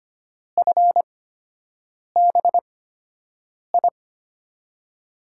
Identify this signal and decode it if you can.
Morse code, heard as a keyed tone, about 25 wpm, spaced as for 7 wpm, 700 Hz. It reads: FBI
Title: USA Call Signs - 6 Characters - 8x Character Spacing - 25wpm